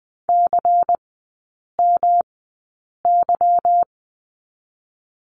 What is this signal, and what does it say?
Morse code, heard as a keyed tone, about 20 wpm, spaced as for 8 wpm, 700 Hz. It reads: CMY
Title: 184 Lesson 30e - Letter b - Three Characters 20wpm